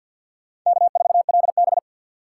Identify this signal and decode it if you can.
Morse code, heard as a keyed tone, about 64 wpm, 690 Hz. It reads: K4LB